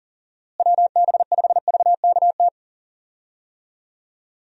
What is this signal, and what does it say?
Morse code, heard as a keyed tone, about 37 wpm, 700 Hz. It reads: WB5VKT
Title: USA Call Signs - 6 Characters - Repeated 3x - 40wpm